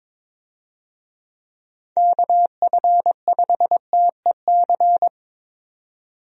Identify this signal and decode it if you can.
Morse code, heard as a keyed tone, about 22 wpm, 700 Hz. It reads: KF5TEC
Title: USA Call Signs - 6 Characters - 22wpm